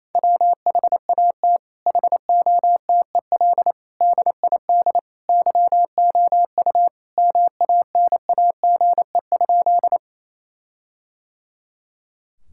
Morse code, heard as a keyed tone, about 28 wpm, 700 Hz. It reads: WHAT HOTEL DID YOU MANAGE?
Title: Sentences from Top 1600 Words 28wpm